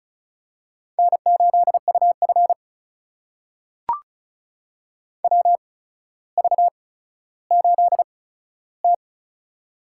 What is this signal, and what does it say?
Morse code, heard as a keyed tone, about 36 wpm, spaced as for 15 wpm, 700 Hz. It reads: N8UF WV8T
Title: USA Call Signs - 4 Characters - 8x Character Spacing - 35wpm